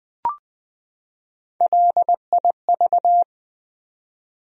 Morse code, heard as a keyed tone, about 20 wpm, 700 Hz. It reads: LIV
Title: USA Common Names - Max 3 Letters - 20wpm